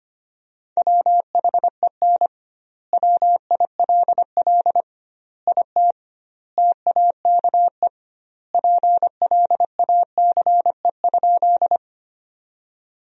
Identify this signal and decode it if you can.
Morse code, heard as a keyed tone, about 25 wpm, 700 Hz. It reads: WHEN WILL IT TAKE PLACE?
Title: 2x Word-Spacing - Sentences from Top 200 Words - 25wpm